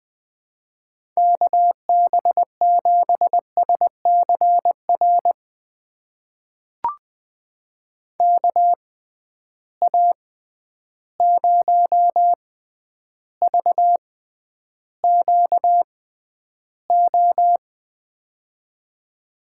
Morse code, heard as a keyed tone, about 20 wpm, spaced as for 10 wpm, 700 Hz. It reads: KB7SCR KA0VQO
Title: USA Call Signs - 6x Character Spacing - 20wpm